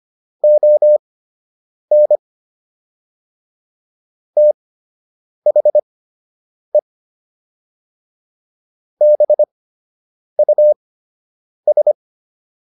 Morse code, heard as a keyed tone, about 25 wpm, spaced as for 8 wpm, 600 Hz. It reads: ON THE BUS